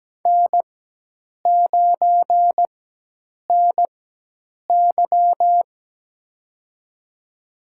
Morse code, heard as a keyed tone, about 17 wpm, spaced as for 8 wpm, 700 Hz. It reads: N9NY
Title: USA Call Signs - 4 Characters - 4x Character Spacing - 17wpm